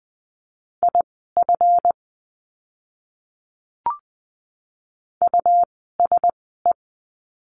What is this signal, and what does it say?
Morse code, heard as a keyed tone, about 20 wpm, spaced as for 14 wpm, 700 Hz. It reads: IF USE
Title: 206 Lesson 33f - Letter j - Words 20wpm